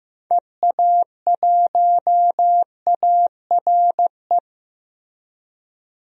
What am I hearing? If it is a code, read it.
Morse code, heard as a keyed tone, about 15 wpm, 700 Hz. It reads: EA1ARE
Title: DXCC-100 Call Signs - 6 Characters - Repeated 2x - 15wpm